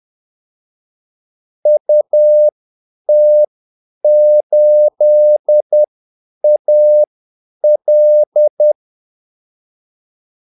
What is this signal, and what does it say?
Morse code, heard as a keyed tone, about 10 wpm, spaced as for 8 wpm, 600 Hz. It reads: UT8AL
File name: Callsigns-10-8wpm